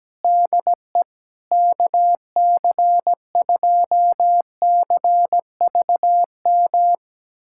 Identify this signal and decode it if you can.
Morse code, heard as a keyed tone, about 17 wpm, 700 Hz. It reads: DE KC2CVM
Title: USA Callsign DE Callsign - 6 Characters - 17wpm